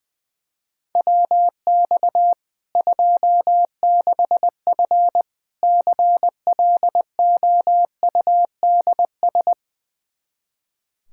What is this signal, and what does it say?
Morse code, heard as a keyed tone, about 20 wpm, 700 Hz. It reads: WX 26F CLOUDS
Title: Practice QSO - Exchange 2 - Info - 20wpm